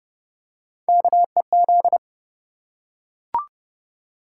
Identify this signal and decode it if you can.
Morse code, heard as a keyed tone, about 28 wpm, 700 Hz. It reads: KEZ